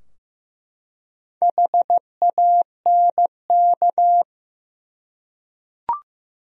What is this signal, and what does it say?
Morse code, heard as a keyed tone, about 15 wpm, 700 Hz. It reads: HANK
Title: USA Common Names - Max 4 Letters - 15wpm